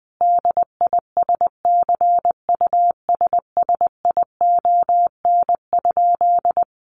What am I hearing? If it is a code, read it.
Morse code, heard as a keyed tone, about 20 wpm, 700 Hz. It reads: DISCUSSION?
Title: Sentences from Top 1100 Words 20wpm